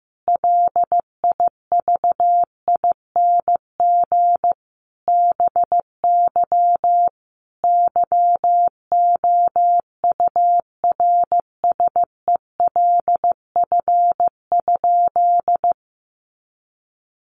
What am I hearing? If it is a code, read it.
Morse code, heard as a keyed tone, about 15 wpm, 700 Hz. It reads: LIVING BY YOURSELF?
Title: Sentences from Top 1100 Words 15wpm